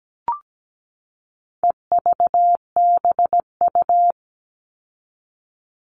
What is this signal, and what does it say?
Morse code, heard as a keyed tone, about 17 wpm, 700 Hz. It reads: EVBU
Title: All Cut Numbers - 4-Digit 17wpm